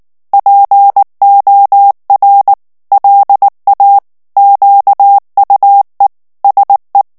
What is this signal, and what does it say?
Morse code, heard as a keyed tone, about 19 wpm, 800 Hz. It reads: POR LA QUE SE
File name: ESPAGNOL TEXTE EN CLAIR 2 VITESSE 18 MOTS MINUTES